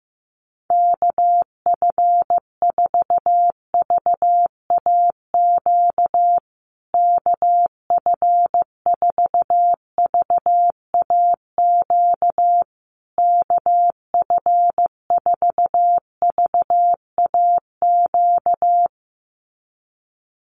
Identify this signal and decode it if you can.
Morse code, heard as a keyed tone, about 15 wpm, 700 Hz. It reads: KF4VAQ KF4VAQ KF4VAQ
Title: USA Call Signs Repeated 3x 15wpm